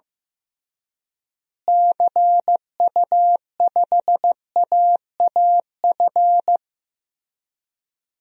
Morse code, heard as a keyed tone, about 15 wpm, 700 Hz. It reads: CU5AAF
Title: DXCC-100 Call Signs - 6 Characters - Repeated 3x - 15wpm